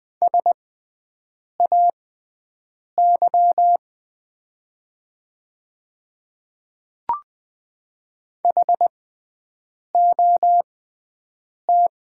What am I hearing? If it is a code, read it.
Morse code, heard as a keyed tone, about 20 wpm, spaced as for 7 wpm, 700 Hz. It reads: SAY HOT
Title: 6x Character Spacing - Top 200 Words - 20wpm